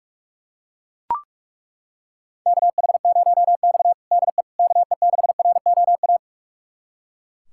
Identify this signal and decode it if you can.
Morse code, heard as a keyed tone, about 41 wpm, 700 Hz. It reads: KH0X DE KE6ROA
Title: USA Callsign DE Callsign 45wpm